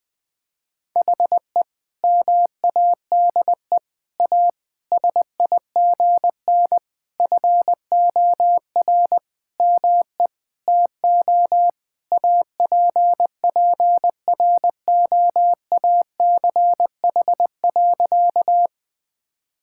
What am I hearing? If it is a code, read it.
Morse code, heard as a keyed tone, about 20 wpm, 700 Hz. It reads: HE MADE A SIGN FOR ME TO APPROACH.
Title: Sentences from Top 1000 Words - Review Part 6 - 20wpm